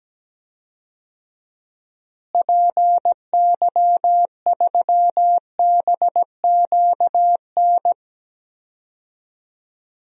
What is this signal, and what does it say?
Morse code, heard as a keyed tone, about 17 wpm, 700 Hz. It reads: PY3BQN
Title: DXCC-100 Call Signs - 6 Characters - Repeated 2x - 17wpm